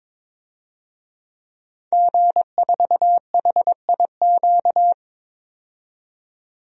Morse code, heard as a keyed tone, about 22 wpm, 700 Hz. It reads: G4HIQ
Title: DXCC-100 Call Signs - 5 Characters - 22wpm